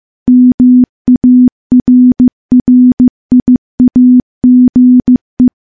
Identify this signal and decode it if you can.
Morse code, heard as a keyed tone, about 15 wpm, 250 Hz. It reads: MARRIAGE
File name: morse-training-15WPM